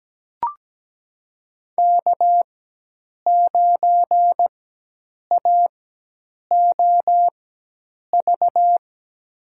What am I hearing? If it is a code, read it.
Morse code, heard as a keyed tone, about 17 wpm, spaced as for 8 wpm, 700 Hz. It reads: K9AOV
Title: USA Call Signs - 5 Characters - 4x Character Spacing - 17wpm